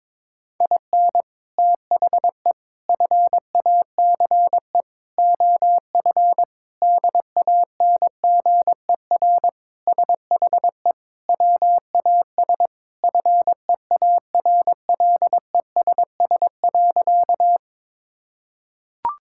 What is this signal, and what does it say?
Morse code, heard as a keyed tone, about 22 wpm, 700 Hz. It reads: IN THE FACE OF DANGER SHE WAS FEARLESS.
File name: practice / 22wpm / Sentences with 4-Word Phrases - 22wpm